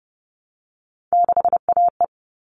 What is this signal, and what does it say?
Morse code, heard as a keyed tone, about 30 wpm, 700 Hz. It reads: 6AE
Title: All Cut Numbers - 3-Digit 30wpm